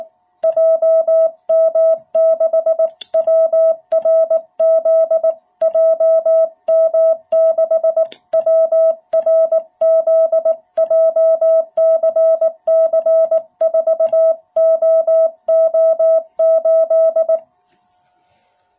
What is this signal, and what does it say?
Morse code, heard as a keyed tone, about 18 wpm, 660 Hz. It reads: JM6WRZ JM6WRZ JCC4OO8
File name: jh8cbh / 2021cw